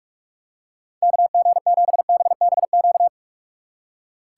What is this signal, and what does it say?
Morse code, heard as a keyed tone, about 46 wpm, 700 Hz. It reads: KG7BBQ
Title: Speed Racing - USA 2x3 Call Signs 45wpm